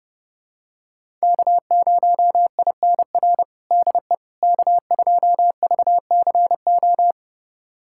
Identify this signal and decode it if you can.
Morse code, heard as a keyed tone, about 30 wpm, 700 Hz. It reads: K0INR DE K2VCO
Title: USA Callsign DE Callsign - 5 Characters - 30wpm